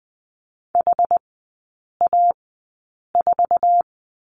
Morse code, heard as a keyed tone, about 19 wpm, spaced as for 8 wpm, 700 Hz. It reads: HA4
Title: 081 Lesson 14e - Number 2 - Three Characters 20wpm